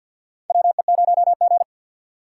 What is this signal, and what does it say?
Morse code, heard as a keyed tone, about 53 wpm, 700 Hz. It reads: WE0G
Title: USA Call Signs - 4x Character Spacing - 50wpm